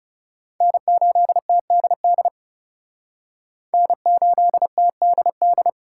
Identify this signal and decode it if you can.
Morse code, heard as a keyed tone, about 37 wpm, 700 Hz. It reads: N8TDD N8TDD
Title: Speed Racing - USA 1x3 Call Signs 30wpm